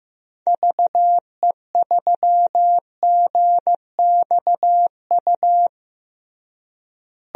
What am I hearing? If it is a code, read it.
Morse code, heard as a keyed tone, about 15 wpm, 700 Hz. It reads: VE3GXU
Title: DXCC-100 Call Signs - 6 Characters - Repeated 2x - 15wpm